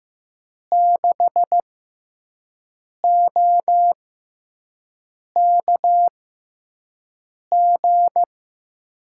Morse code, heard as a keyed tone, about 15 wpm, spaced as for 5 wpm, 700 Hz. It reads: 6OKG
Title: USA Call Signs - 5 Characters - 6x Character Spacing - 15wpm